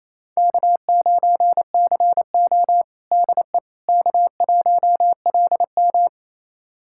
Morse code, heard as a keyed tone, about 28 wpm, 700 Hz. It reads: K9CO DE K1LM
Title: USA Callsign DE Callsign - 4 Characters - 28wpm